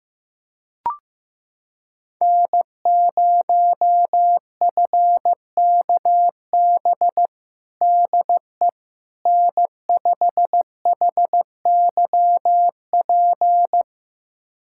Morse code, heard as a keyed tone, about 15 wpm, 700 Hz. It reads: N0FKB DE N5HYP
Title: USA Callsign DE Callsign - 5 Characters - 15wpm